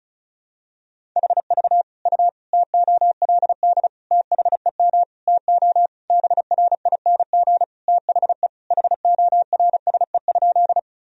Morse code, heard as a keyed tone, about 34 wpm, 700 Hz. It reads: HV U TOLD THEM TO BRING THE HORSE?
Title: Sentences from Top 500 Words and Abbreviations 35wpm